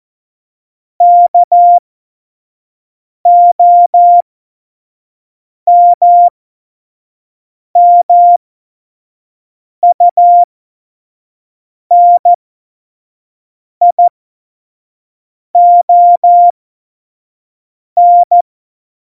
Morse code, heard as a keyed tone, about 14 wpm, spaced as for 5 wpm, 700 Hz. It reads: KOMMUNION
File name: Morsekurs / Witze / Witz 05. Januar 2021, 25 F70